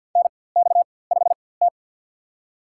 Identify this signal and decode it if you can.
Morse code, heard as a keyed tone, about 51 wpm, spaced as for 23 wpm, 700 Hz. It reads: NX5T
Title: USA Call Signs - 4x Character Spacing - 50wpm